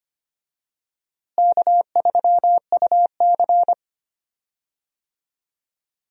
Morse code, heard as a keyed tone, about 25 wpm, 700 Hz. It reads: K3UC
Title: USA Call Signs - 4 Characters - Repeated 2x - 25wpm